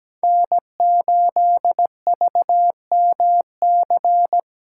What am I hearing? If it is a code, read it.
Morse code, heard as a keyed tone, about 17 wpm, 700 Hz. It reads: N8VMC
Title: USA Callsign DE Callsign - 5 Characters - 17wpm